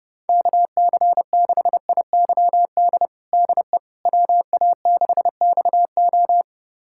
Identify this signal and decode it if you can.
Morse code, heard as a keyed tone, about 30 wpm, 700 Hz. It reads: KC6IYD DE WA6XO
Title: USA Callsign DE Callsign 30wpm